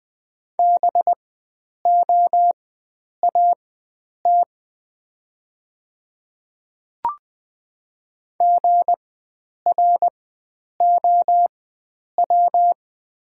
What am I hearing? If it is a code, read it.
Morse code, heard as a keyed tone, about 20 wpm, spaced as for 9 wpm, 690 Hz. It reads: BOAT GROW
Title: 4x Character Spacing - Top 500 Words - 20wpm